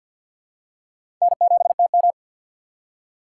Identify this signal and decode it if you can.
Morse code, heard as a keyed tone, about 53 wpm, 700 Hz. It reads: N7TM
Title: USA Call Signs - 2x Character Spacing - 50wpm